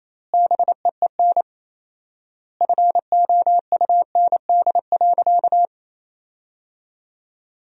Morse code, heard as a keyed tone, about 28 wpm, 700 Hz. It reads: BEEN FOUND.
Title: 4x Word-Spacing - Sentences from Top 300 Words - 28wpm